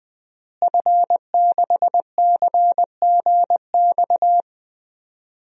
Morse code, heard as a keyed tone, about 20 wpm, 700 Hz. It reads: F6CGX
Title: DXCC-100 Call Signs - 5 Characters - Repeated 3x - 20wpm